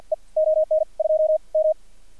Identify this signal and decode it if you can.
Morse code, heard as a keyed tone, about 56 wpm, spaced as for 41 wpm, 620 Hz. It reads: E ON 2M
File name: archived / CRSnet